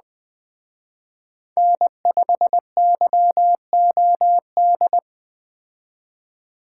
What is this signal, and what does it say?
Morse code, heard as a keyed tone, about 20 wpm, 700 Hz. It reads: N5YOD